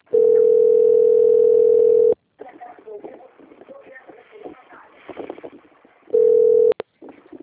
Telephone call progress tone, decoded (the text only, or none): ringback tone